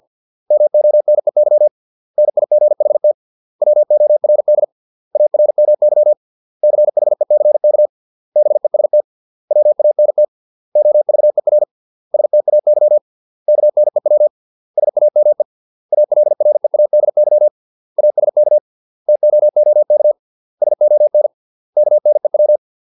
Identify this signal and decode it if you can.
Morse code, heard as a keyed tone, about 50 wpm, 600 Hz. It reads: MONEY NIGHT WORD ARMY CHECK BEST WANT OVER STAY KNEW SAME ALREADY ASK TOOK SON KNEW